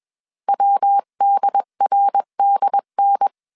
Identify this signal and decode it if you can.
Morse code, heard as a keyed tone, about 21 wpm, 800 Hz. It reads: WDRDN
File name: 20wpm